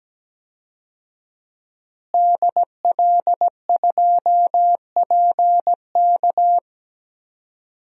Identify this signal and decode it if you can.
Morse code, heard as a keyed tone, about 17 wpm, 700 Hz. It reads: DL2PK
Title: DXCC-100 Call Signs - 5 Characters - Repeated 3x - 17wpm